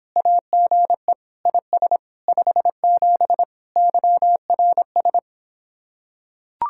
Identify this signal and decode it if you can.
Morse code, heard as a keyed tone, about 26 wpm, 700 Hz. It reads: AGE IS 57 YRS